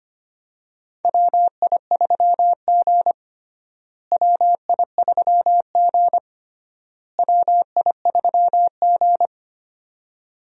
Morse code, heard as a keyed tone, about 25 wpm, 700 Hz. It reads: WI3G WI3G WI3G
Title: USA Call Signs - 4 Characters - Repeated 3x - 25wpm